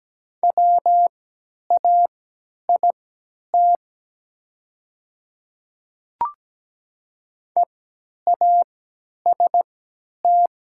Morse code, heard as a keyed tone, about 17 wpm, spaced as for 10 wpm, 700 Hz. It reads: WAIT EAST
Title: 3x Character Spacing - Top 500 Words - 17wpm